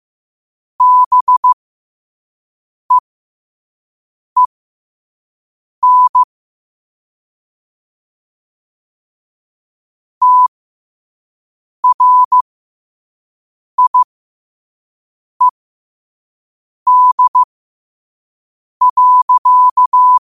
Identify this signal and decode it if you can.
Morse code, heard as a keyed tone, about 15 wpm, spaced as for 5 wpm, 1000 Hz. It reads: BEEN TRIED.
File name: qotdfiles / 5wpm